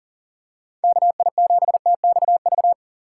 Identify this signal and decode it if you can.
Morse code, heard as a keyed tone, about 40 wpm, 700 Hz. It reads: KI7TXV
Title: USA Call Signs - 6 Characters - 3x Character Spacing - 40wpm